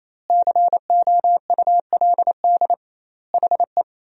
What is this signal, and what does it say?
Morse code, heard as a keyed tone, about 28 wpm, 700 Hz. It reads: COULD HE
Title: 2x Word-Spacing - Sentences from Top 200 Words - 28wpm